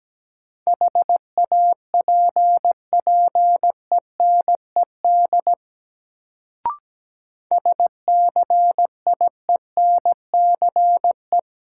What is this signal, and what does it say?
Morse code, heard as a keyed tone, about 17 wpm, 700 Hz. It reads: HAPPENED SCIENCE
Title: Sentences from Top 1000 Words - Review Part 2 - New Word Drill - 17wpm